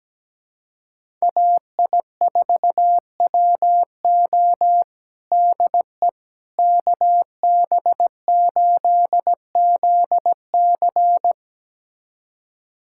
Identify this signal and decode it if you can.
Morse code, heard as a keyed tone, about 17 wpm, 700 Hz. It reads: AI4WO DE KB8ZC
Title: USA Callsign DE Callsign - 5 Characters - 17wpm